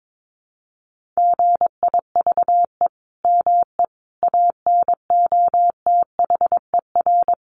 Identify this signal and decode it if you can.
Morse code, heard as a keyed tone, about 22 wpm, 700 Hz. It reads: GIVE ME ANOTHER
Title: Sentences from Top 1400 Words 22wpm